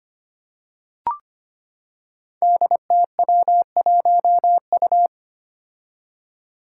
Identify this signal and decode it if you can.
Morse code, heard as a keyed tone, about 25 wpm, 700 Hz. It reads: DTW1U
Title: Five Challenging Letter-Number Combinations 25wpm